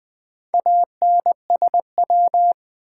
Morse code, heard as a keyed tone, about 20 wpm, 700 Hz. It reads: ANSW
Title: Build A Word - 2 Letters At A Time - 6-Letter Words - 20wpm